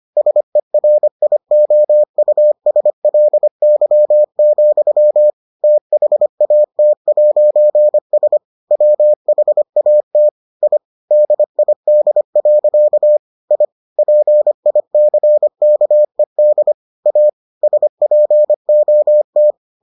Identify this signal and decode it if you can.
Morse code, heard as a keyed tone, about 25 wpm, 600 Hz. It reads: SERIOUSLY, THAT'S WHAT I DID. I PICKED A SPOT